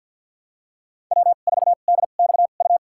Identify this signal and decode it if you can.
Morse code, heard as a keyed tone, about 47 wpm, spaced as for 35 wpm, 700 Hz. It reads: W4DXU
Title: USA Call Signs - 5 Characters - 2x Character Spacing - 50wpm